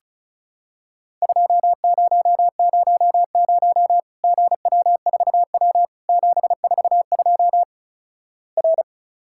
Morse code, heard as a keyed tone, about 36 wpm, 700 Hz. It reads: 2000 GW4W 742 R
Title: Worked All Europe DX Contest - QTC Exchange - 35wpm